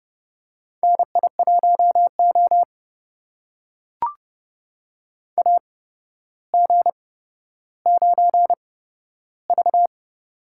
Morse code, heard as a keyed tone, about 30 wpm, spaced as for 13 wpm, 700 Hz. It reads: NI1O AG9V
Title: USA Call Signs - 4 Characters - 8x Character Spacing - 30wpm